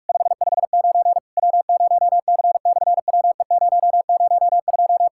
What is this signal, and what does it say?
Morse code, heard as a keyed tone, about 44 wpm, 700 Hz. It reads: 559 W0C/WE002
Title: SOTA QSO Practice 45wpm